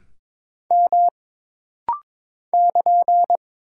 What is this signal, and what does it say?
Morse code, heard as a keyed tone, about 22 wpm, 700 Hz. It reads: M <KN>
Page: This rapid-fire timing is geared to push your ICR (Instant Character Recognition) skills.